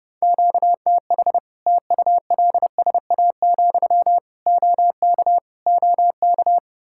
Morse code, heard as a keyed tone, about 30 wpm, 700 Hz. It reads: QTH TULSA, OK OK
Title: Practice QSO - Exchange 1 - Beginning - 30wpm